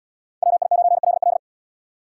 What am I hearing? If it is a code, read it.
Morse code, heard as a keyed tone, about 73 wpm, 700 Hz. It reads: WI9LL